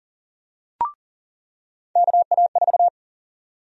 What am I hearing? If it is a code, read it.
Morse code, heard as a keyed tone, about 40 wpm, 700 Hz. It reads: KA4
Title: USA Call Sign Prefixes - 40wpm